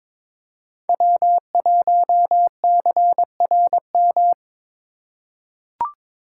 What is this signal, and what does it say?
Morse code, heard as a keyed tone, about 22 wpm, 700 Hz. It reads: W1CRM
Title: USA Call Signs - 2x Character Spacing - 22wpm